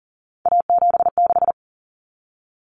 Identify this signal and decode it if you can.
Morse code, heard as a keyed tone, about 40 wpm, 700 Hz. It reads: A76
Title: Common Cut Numbers - 3-Digit 40wpm